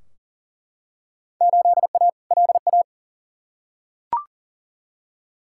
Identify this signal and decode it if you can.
Morse code, heard as a keyed tone, about 40 wpm, 700 Hz. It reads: 8A LA
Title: Field Day Exchanges Encore - 40wpm